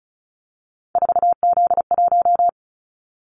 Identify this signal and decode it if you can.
Morse code, heard as a keyed tone, about 36 wpm, 700 Hz. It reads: 4Z1
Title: Callsign Prefixes for Top 50 DXCC - 35wpm